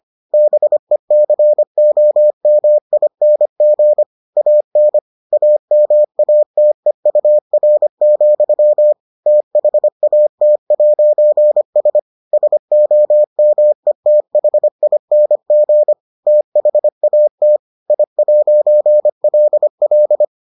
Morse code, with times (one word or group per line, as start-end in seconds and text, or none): 0.3-4.0 BECOMING
4.4-5.0 AN
5.3-8.9 AMATEUR,
9.3-12.0 THAT'S
12.3-15.9 SOMETHING
16.3-17.6 THAT
17.9-20.2 I'LL